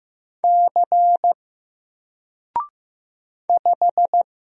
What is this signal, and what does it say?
Morse code, heard as a keyed tone, about 15 wpm, 700 Hz. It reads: C 5
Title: Single Character (Letter:Number:Punctuation:Prosign) 15wpm